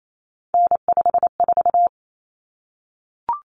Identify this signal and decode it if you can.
Morse code, heard as a keyed tone, about 28 wpm, 700 Hz. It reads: N54
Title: Common Cut Numbers - 3-Digit 28wpm